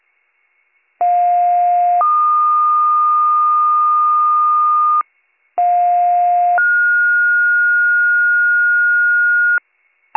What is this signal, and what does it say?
Two-tone page, heard as a one-second tone, then a three-second tone, about 710 then 1220 Hz, about 710 then 1430 Hz, with noise. It is heard from 1.0 to 9.6 s.